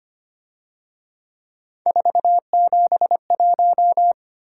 Morse code, heard as a keyed tone, about 31 wpm, 700 Hz. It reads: 471
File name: Speed-Racing - Three Letter-Number Combinations - 25wpm